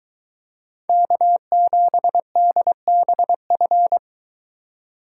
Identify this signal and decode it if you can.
Morse code, heard as a keyed tone, about 23 wpm, 700 Hz. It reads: K7DBF